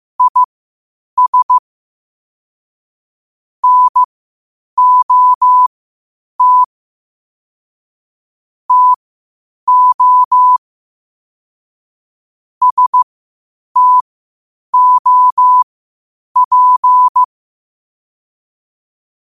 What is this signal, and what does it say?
Morse code, heard as a keyed tone, about 15 wpm, spaced as for 8 wpm, 1000 Hz. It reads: IS NOT TO STOP
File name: qotdfiles / 8wpm